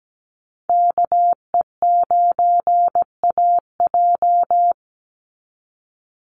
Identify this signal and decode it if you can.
Morse code, heard as a keyed tone, about 17 wpm, 700 Hz. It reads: KE9AJ
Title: NA SOTA Chaser Callsigns 17wpm